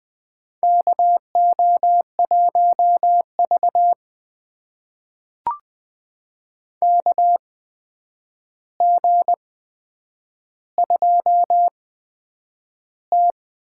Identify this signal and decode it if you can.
Morse code, heard as a keyed tone, about 20 wpm, spaced as for 9 wpm, 700 Hz. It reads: KO1V KG2T
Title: USA Call Signs - 4 Characters - 8x Character Spacing - 20wpm